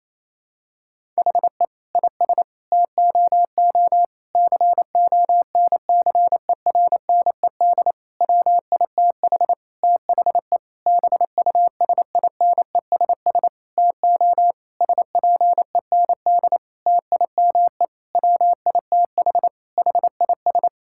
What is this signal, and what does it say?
Morse code, heard as a keyed tone, about 28 wpm, 700 Hz. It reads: HE IS TOO CONCERNED WITH THE BUSINESS TO SPEND TIME WITH HIS
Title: Sentences from Top 1000 Words - Review Part 6 - 28wpm